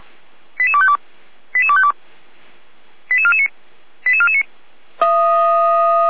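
Five-tone selective call (ZVEI), heard as quick runs of five tones, tones of about 70 ms. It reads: 88252 88252 88449 88449
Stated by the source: ZVEI five-tone call 88191 received by SDR, channel 455.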